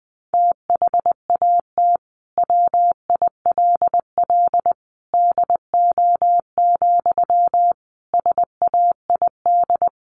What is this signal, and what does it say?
Morse code, heard as a keyed tone, about 20 wpm, 700 Hz. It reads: THAT WILL DO, SAID
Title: Sentences from Top 100 Words 20wpm